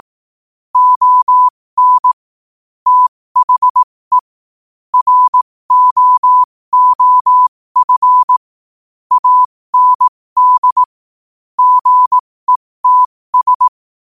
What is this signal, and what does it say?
Morse code, heard as a keyed tone, about 18 wpm, spaced as for 15 wpm, 1000 Hz. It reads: ON THE ROOF AND GETS
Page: qotd-at-15-wpm